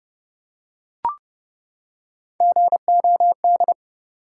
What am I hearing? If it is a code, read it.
Morse code, heard as a keyed tone, about 32 wpm, 700 Hz. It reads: GOD
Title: Three Letter Words - 30wpm